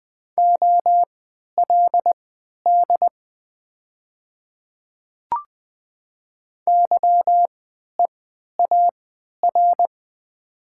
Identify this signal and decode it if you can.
Morse code, heard as a keyed tone, about 20 wpm, spaced as for 11 wpm, 700 Hz. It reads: OLD YEAR
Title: 3x Character Spacing - Top 200 Words - 20wpm